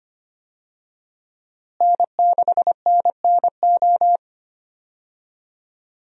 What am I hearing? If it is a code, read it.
Morse code, heard as a keyed tone, about 25 wpm, 700 Hz. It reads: N6NNO
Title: USA Call Signs - 5 Characters - Repeated 3x - 25wpm